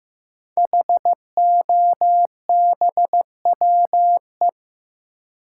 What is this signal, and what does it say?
Morse code, heard as a keyed tone, about 15 wpm, 700 Hz. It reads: HOBWE
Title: Code Groups - 5 Letters - 15wpm